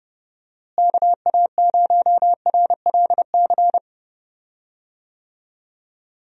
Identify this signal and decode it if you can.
Morse code, heard as a keyed tone, about 31 wpm, 700 Hz. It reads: KA0RLC
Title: USA Call Signs - 6 Characters - Repeated 3x - 30wpm